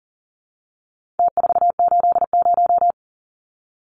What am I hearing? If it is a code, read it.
Morse code, heard as a keyed tone, about 40 wpm, 700 Hz. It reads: T480
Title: Numbers - Contest Style - 4-Digit 40wpm